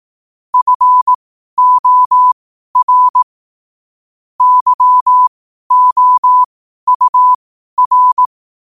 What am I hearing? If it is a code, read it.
Morse code, heard as a keyed tone, about 18 wpm, spaced as for 12 wpm, 1000 Hz. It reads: FOR YOUR
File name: qotdfiles / 12wpm